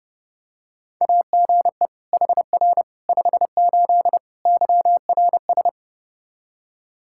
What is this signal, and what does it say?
Morse code, heard as a keyed tone, about 32 wpm, 700 Hz. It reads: AGE HR 58 YRS
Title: Practice QSO - Exchange 3 - More Info - 30wpm